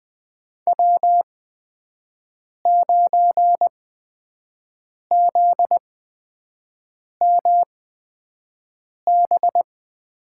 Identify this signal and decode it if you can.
Morse code, heard as a keyed tone, about 20 wpm, spaced as for 5 wpm, 700 Hz. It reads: W9ZMB
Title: USA Call Signs - 5 Characters - 8x Character Spacing - 20wpm